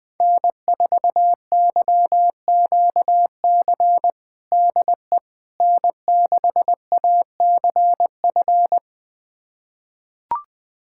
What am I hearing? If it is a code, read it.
Morse code, heard as a keyed tone, about 20 wpm, 700 Hz. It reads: N4YQC DE N6ACF
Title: USA Callsign DE Callsign - 5 Characters - 20wpm